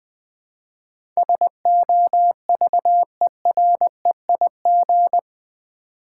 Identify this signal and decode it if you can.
Morse code, heard as a keyed tone, about 20 wpm, 700 Hz. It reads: SOVEREIG
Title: Build A Word - 4 Letters At A Time - 11-Letter Words - 20wpm